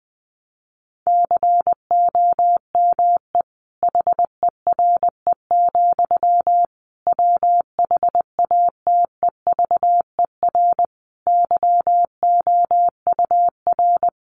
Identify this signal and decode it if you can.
Morse code, heard as a keyed tone, about 20 wpm, 700 Hz. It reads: COME HERE, WHATEVER YOUR
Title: Sentences from Top 1100 Words 20wpm